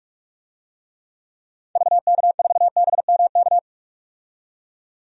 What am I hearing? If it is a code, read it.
Morse code, heard as a keyed tone, about 42 wpm, 700 Hz. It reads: VK4BMK
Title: DXCC-100 Call Signs - 6 Characters - 45wpm